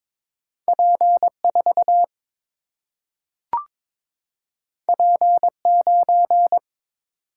Morse code, heard as a keyed tone, about 22 wpm, 700 Hz. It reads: P 4 P 9
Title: Single-Letter Single-Number 22wpm